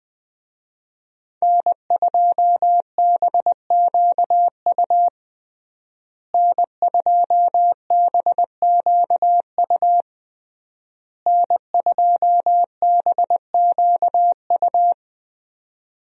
Morse code, heard as a keyed tone, about 20 wpm, 700 Hz. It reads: N2BQU N2BQU N2BQU
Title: USA Call Signs - 5 Characters - Repeated 3x - 20wpm